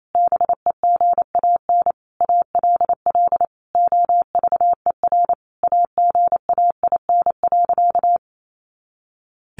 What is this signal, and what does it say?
Morse code, heard as a keyed tone, about 28 wpm, 700 Hz. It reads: BEGAN ALL OVER AGAIN.
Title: Sentences from Top 500 Words 28wpm